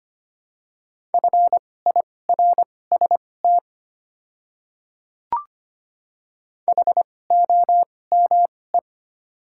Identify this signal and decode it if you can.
Morse code, heard as a keyed tone, about 25 wpm, spaced as for 18 wpm, 690 Hz. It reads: FIRST HOME
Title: 2x Character Spacing - Top 200 Words - 25wpm